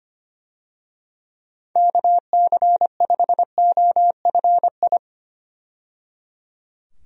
Morse code, heard as a keyed tone, about 25 wpm, 700 Hz. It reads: KC5OFI
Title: USA Call Signs - 6 Characters - 25wpm